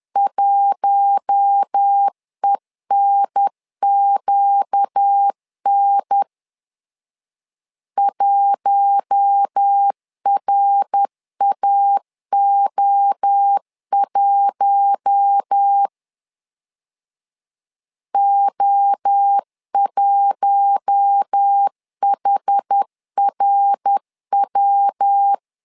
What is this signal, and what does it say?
Morse code, heard as a keyed tone, about 10 wpm, 800 Hz. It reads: 1ENQN 1RAO1 O1HRW
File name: LEZIONI / 4 Q1 10wpm 2448